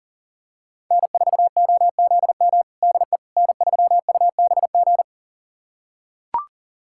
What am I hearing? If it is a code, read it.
Morse code, heard as a keyed tone, about 40 wpm, 700 Hz. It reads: N4OZM DE N3UBG
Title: USA Callsign DE Callsign - 5 Characters - 40wpm